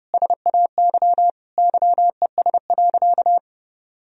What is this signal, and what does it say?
Morse code, heard as a keyed tone, about 30 wpm, 700 Hz. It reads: SAY YES.
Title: Sentences from Top 500 Words and Abbreviations 30wpm